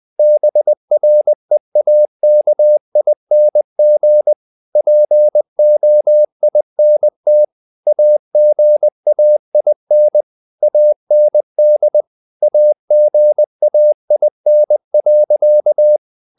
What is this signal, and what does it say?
Morse code, heard as a keyed tone, about 20 wpm, 600 Hz. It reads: BREAKING POINT AGAIN AND AGAIN.
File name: the old man and the sea part 2 20wpm